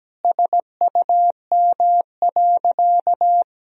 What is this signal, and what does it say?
Morse code, heard as a keyed tone, about 17 wpm, 700 Hz. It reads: SUM.
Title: Sentences from Top 100 Words and Abbreviations 17wpm